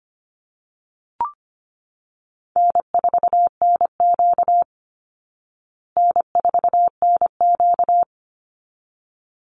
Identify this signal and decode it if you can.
Morse code, heard as a keyed tone, about 25 wpm, 700 Hz. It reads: N4NQ N4NQ
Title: Easy Call Signs Repeated 2x - 25wpm